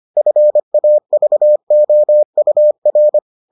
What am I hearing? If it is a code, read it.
Morse code, heard as a keyed tone, about 25 wpm, 600 Hz. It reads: FAVOUR